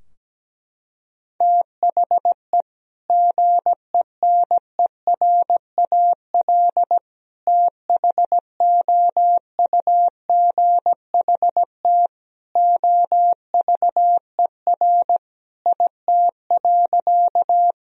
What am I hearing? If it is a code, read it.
Morse code, heard as a keyed tone, about 17 wpm, 700 Hz. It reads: THE GENERAL THOUGHT OVER IT.
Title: Sentences from Top 1000 Words - Review Part 3 - 17wpm